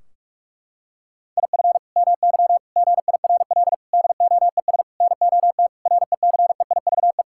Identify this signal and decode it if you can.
Morse code, heard as a keyed tone, about 42 wpm, 700 Hz. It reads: IF MY GIRL DOES NOT RECEIVE
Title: Sentences from Top 1600 Words 45wpm